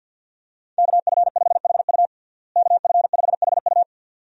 Morse code, heard as a keyed tone, about 47 wpm, 700 Hz. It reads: KF5HU KF5HU
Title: USA Call Signs - 5 Characters - Repeated 2x - 50wpm